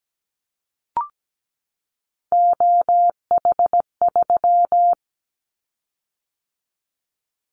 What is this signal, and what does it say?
Morse code, heard as a keyed tone, about 17 wpm, 700 Hz. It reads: OH3
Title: Callsign Prefixes for Top 25 DXCC - 17wpm